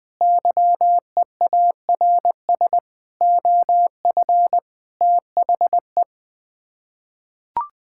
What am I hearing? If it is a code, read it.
Morse code, heard as a keyed tone, about 21 wpm, 700 Hz. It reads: YEARS OF THE
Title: Sets of 6 Words - 20wpm